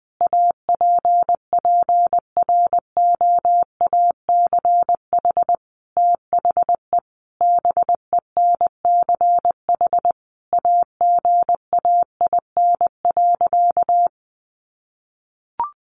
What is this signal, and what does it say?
Morse code, heard as a keyed tone, about 20 wpm, 700 Hz. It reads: APPROACH THE BENCH AGAIN.
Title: Sentences from Top 1400 Words 20wpm